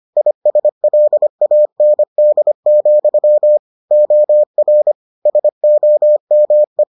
Morse code, heard as a keyed tone, about 25 wpm, 600 Hz. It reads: ISLAND, OR SOME